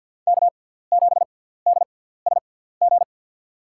Morse code, heard as a keyed tone, about 47 wpm, spaced as for 17 wpm, 700 Hz. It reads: K7DSG
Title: USA Call Signs - 5 Characters - 6x Character Spacing - 50wpm